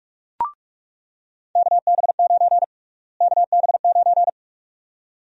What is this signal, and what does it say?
Morse code, heard as a keyed tone, about 46 wpm, 700 Hz. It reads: KB9 KB9
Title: USA Call Sign Prefixes - Repeated 2x - 45wpm